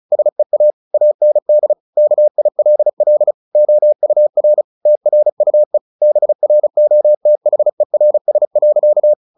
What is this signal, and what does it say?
Morse code, heard as a keyed tone, about 34 wpm, 600 Hz. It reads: SEA AND KILL OUR TRUE BROTHERS.
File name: the old man and the sea 2 35wpm